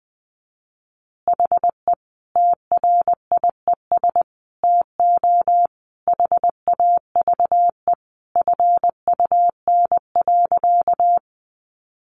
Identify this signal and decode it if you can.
Morse code, heard as a keyed tone, about 20 wpm, 700 Hz. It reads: HE TRIES TO HAVE FUN.
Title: Easy Phrases 20wpm